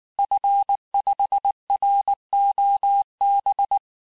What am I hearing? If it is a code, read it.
Morse code, heard as a keyed tone, about 19 wpm, 800 Hz. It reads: F5ROB
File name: cours-du-lundi files / cours-1583-fin-QSO-18-mpm